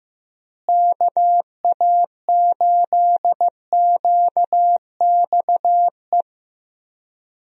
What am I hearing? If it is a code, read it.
Morse code, heard as a keyed tone, about 15 wpm, 700 Hz. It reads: KA8QXE
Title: USA Call Signs - 6 Characters - Repeated 3x - 15wpm